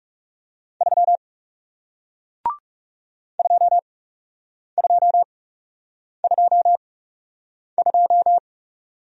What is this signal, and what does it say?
Morse code, heard as a keyed tone, about 39 wpm, spaced as for 8 wpm, 700 Hz. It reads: 3 2222